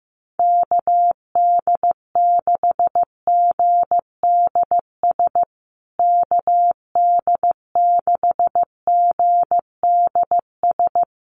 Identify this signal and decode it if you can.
Morse code, heard as a keyed tone, about 15 wpm, 700 Hz. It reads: KD6GDS KD6GDS
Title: USA Call Signs Repeated 3x 15wpm